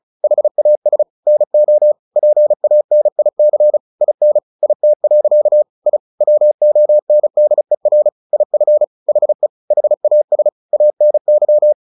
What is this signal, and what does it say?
Morse code, heard as a keyed tone, about 34 wpm, 600 Hz. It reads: HAS NO PANIC IN IT. I WONDER IF HE HAS ANY
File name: the old man and the sea / the old man and the sea 1 35wpm